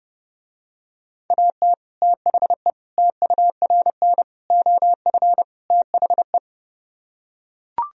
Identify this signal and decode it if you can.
Morse code, heard as a keyed tone, about 30 wpm, 700 Hz. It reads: AT THE TURN OF THE